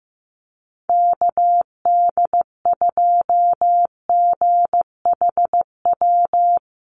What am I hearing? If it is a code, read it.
Morse code, heard as a keyed tone, about 15 wpm, 700 Hz. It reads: KD2GHW
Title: USA Call Signs Repeated 3x 15wpm